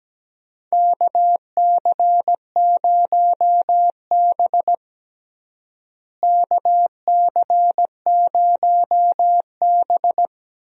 Morse code, heard as a keyed tone, about 17 wpm, 700 Hz. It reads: KC0B KC0B
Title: USA Call Signs - 4 Characters - Repeated 2x - 17wpm